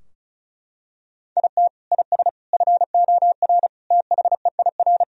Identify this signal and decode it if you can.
Morse code, heard as a keyed tone, about 36 wpm, 700 Hz. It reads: IT IS FOR THEIR
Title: Sentences from Top 1000 Words - Review Part 4 - 35wpm